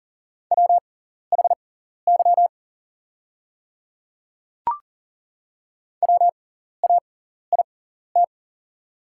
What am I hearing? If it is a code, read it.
Morse code, heard as a keyed tone, about 40 wpm, spaced as for 14 wpm, 700 Hz. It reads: WHY WAIT
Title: 6x Character Spacing - Top 500 Words - 40wpm